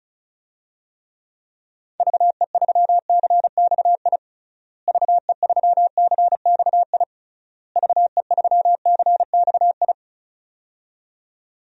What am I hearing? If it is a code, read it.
Morse code, heard as a keyed tone, about 34 wpm, 700 Hz. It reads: VE3CXI VE3CXI VE3CXI
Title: DXCC-100 Call Signs - 6 Characters - Repeated 3x - 35wpm